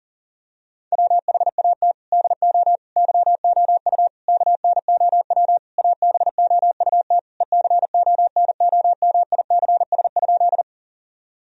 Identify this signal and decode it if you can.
Morse code, heard as a keyed tone, about 40 wpm, 700 Hz. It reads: WHAT DO YOU KNOW ABOUT ECONOMICS?
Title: Sentences from Top 1000 Words - Review Part 3 - 40wpm